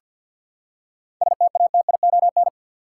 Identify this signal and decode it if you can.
Morse code, heard as a keyed tone, about 51 wpm, 700 Hz. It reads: STATION
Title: Seven Letter Words - 50wpm